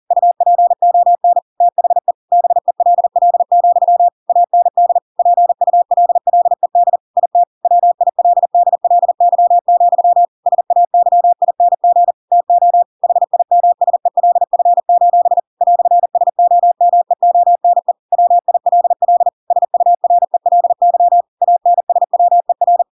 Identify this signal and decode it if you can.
Morse code, heard as a keyed tone, about 40 wpm, 700 Hz. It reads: UPON THE BELL, AND PULLED IT WILDLY, SAYING TO HIMSELF: "SOMEONE WILL SURELY ANSWER
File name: Pinocchio / Pinocchio ch6 40 40 ik1hge